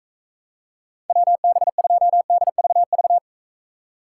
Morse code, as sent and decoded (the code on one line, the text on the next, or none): .-- -... ..--- -.. ...- ...-
WB2DVV